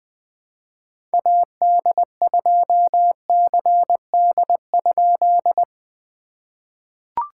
Morse code, heard as a keyed tone, about 20 wpm, 700 Hz. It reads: AD2CD?
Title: POTA QSO Practice - 20wpm